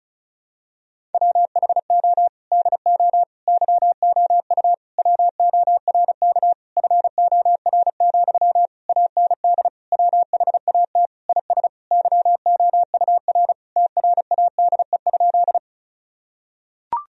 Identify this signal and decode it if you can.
Morse code, heard as a keyed tone, about 34 wpm, 700 Hz. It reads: WHO DO YOU WORK FOR, AND WHAT IS YOUR TRADE?
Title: Sentences from Top 1000 Words - Review Part 4 - 35wpm